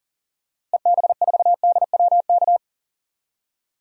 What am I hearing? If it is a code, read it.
Morse code, heard as a keyed tone, about 37 wpm, 700 Hz. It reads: EB4DWK
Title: DXCC-100 Call Signs - 6 Characters - Repeated 3x - 40wpm